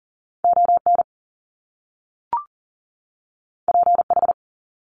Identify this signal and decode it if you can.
Morse code, heard as a keyed tone, about 41 wpm, 700 Hz. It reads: ON PH
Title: Top Two Letter Combinations in Words 40wpm